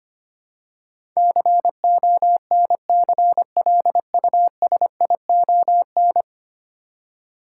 Morse code, heard as a keyed tone, about 25 wpm, 700 Hz. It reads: CONCLUSION
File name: Ten Letter Words - 25wpm